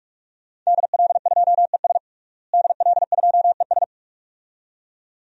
Morse code, heard as a keyed tone, about 46 wpm, 700 Hz. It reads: DL2ES DL2ES
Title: DXCC-100 Call Signs - 5 Characters - Repeated 2x - 45wpm